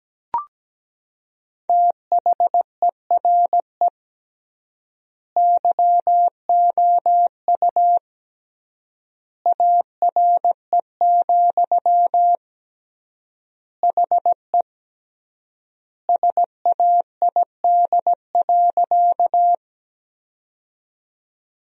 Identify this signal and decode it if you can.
Morse code, heard as a keyed tone, about 17 wpm, 700 Hz. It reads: THERE YOU ARE, HE SAID.
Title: 3x Word-Spacing - Sentences from Top 100 Words - 17wpm